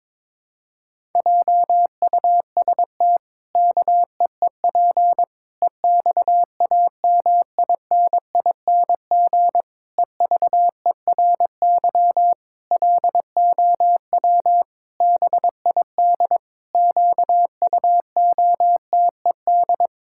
Morse code, heard as a keyed tone, about 22 wpm, 700 Hz. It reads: JUST KEEP EXAMINING EVERY LOW BID QUOTED
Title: Pangram Challenge v2 - 22wpm